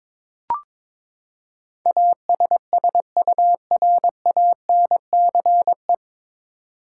Morse code, heard as a keyed tone, about 22 wpm, 700 Hz. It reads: ASSURANCE
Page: Morse Code Ninja practice 22wpm